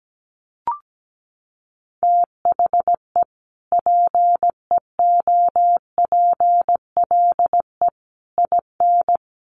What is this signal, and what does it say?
Morse code, heard as a keyed tone, about 17 wpm, 700 Hz. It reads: THE PEOPLE IN
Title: Sentences from Top 1400 Words 17wpm